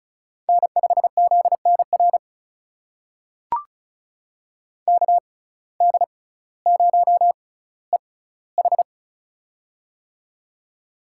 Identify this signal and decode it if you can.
Morse code, heard as a keyed tone, about 34 wpm, spaced as for 18 wpm, 700 Hz. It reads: N5ZNR KD0EH
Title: USA Call Signs - 5 Characters - 6x Character Spacing - 35wpm